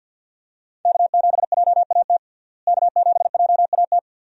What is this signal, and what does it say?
Morse code, heard as a keyed tone, about 49 wpm, 700 Hz. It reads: K7JAT K7JAT
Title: USA Call Signs - 5 Characters - Repeated 2x - 50wpm